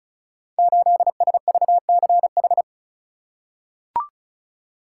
Morse code, heard as a keyed tone, about 36 wpm, 700 Hz. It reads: 8SVCH
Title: Code Groups - 5 Letter:Numbers - 35wpm